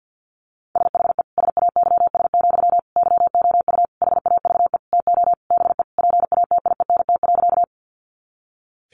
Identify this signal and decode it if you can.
Morse code, heard as a keyed tone, about 50 wpm, 700 Hz. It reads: SHE SAYS, YOU HAVE TO BE PATIENT.
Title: Easy Phrases 50wpm